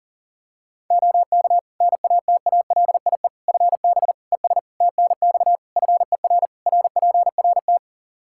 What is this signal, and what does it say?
Morse code, heard as a keyed tone, about 40 wpm, 700 Hz. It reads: OK NATALIE FB ES TNX FER RPRT
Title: Practice QSO - Exchange 2 - Info - 40wpm